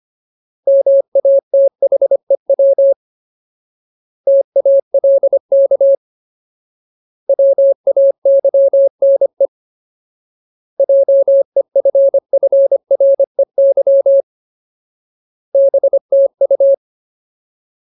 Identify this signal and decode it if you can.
Morse code, heard as a keyed tone, about 25 wpm, 550 Hz. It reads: MATHEW TALK WAYNE JEFFREY BTU
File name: Ultimate Word List / 25WPM Ultimate